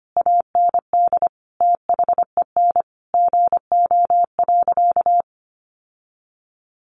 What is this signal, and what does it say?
Morse code, heard as a keyed tone, about 25 wpm, 700 Hz. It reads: AND THEN GO.
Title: Sentences from Top 1300 Words 25wpm